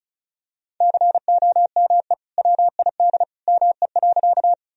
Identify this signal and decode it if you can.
Morse code, heard as a keyed tone, about 34 wpm, 700 Hz. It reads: COME WID ME.
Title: Sentences from Top 200 Words and Abbreviations 35wpm